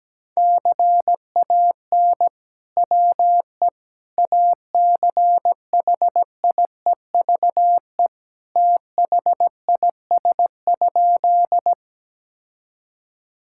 Morse code, heard as a keyed tone, about 17 wpm, 700 Hz. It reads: CAN WE ACHIEVE THIS?
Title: Sentences from Top 1000 Words - Review Part 6 - 17wpm